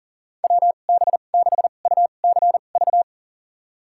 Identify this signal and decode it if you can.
Morse code, heard as a keyed tone, about 40 wpm, spaced as for 29 wpm, 700 Hz. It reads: WB6UCV
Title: USA Call Signs - 6 Characters - 2x Character Spacing - 40wpm